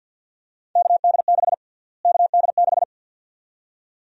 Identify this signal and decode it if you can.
Morse code, heard as a keyed tone, about 52 wpm, 700 Hz. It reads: KD6 KD6
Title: USA Call Sign Prefixes - Repeated 2x - 50wpm